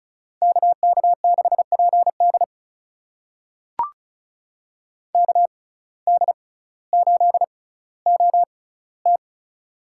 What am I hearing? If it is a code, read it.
Morse code, heard as a keyed tone, about 36 wpm, spaced as for 18 wpm, 700 Hz. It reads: KK6PD KD8OT
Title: USA Call Signs - 5 Characters - 6x Character Spacing - 35wpm